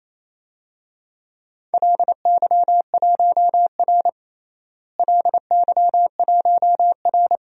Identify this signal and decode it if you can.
Morse code, heard as a keyed tone, about 28 wpm, 700 Hz. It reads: LY1R LY1R
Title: DXCC-100 Call Signs - 4 Characters - Repeated 2x - 28wpm